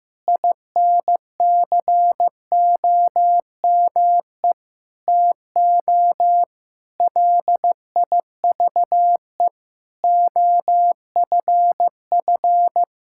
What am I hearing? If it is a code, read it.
Morse code, heard as a keyed tone, about 15 wpm, 700 Hz. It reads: INCOME TO LIVE OFF
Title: Sentences from Top 1000 Words - Review Part 5 - 15wpm